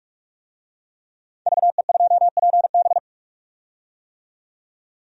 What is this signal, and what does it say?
Morse code, heard as a keyed tone, about 46 wpm, 700 Hz. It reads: VE2PB
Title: DXCC-100 Call Signs - 5 Characters - 45wpm